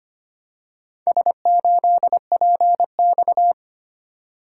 Morse code, heard as a keyed tone, about 25 wpm, 700 Hz. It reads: S8PX
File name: practice / 25wpm / Four Letter-Number Combinations - 25wpm